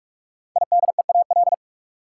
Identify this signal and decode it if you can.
Morse code, heard as a keyed tone, about 46 wpm, 700 Hz. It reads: IDEAL
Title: Sentences from Top 1400 Words - New Word Drill - 45wpm